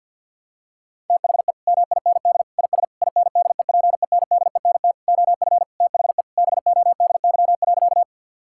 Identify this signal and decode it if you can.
Morse code, heard as a keyed tone, about 51 wpm, 700 Hz. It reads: THE MIND IS INDEPENDENT OF THE BODY.
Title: Sentences from Top 1000 Words - Review Part 6 - 50wpm